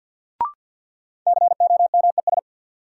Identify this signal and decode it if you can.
Morse code, heard as a keyed tone, about 53 wpm, 700 Hz. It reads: COMES
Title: Sets of 3 Words - New Word Drill - 50wpm